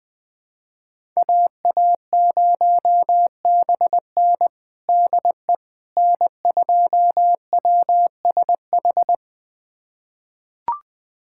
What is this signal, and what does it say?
Morse code, heard as a keyed tone, about 20 wpm, 700 Hz. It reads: AA0BN DE N2WSH
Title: USA Callsign DE Callsign - 5 Characters - 20wpm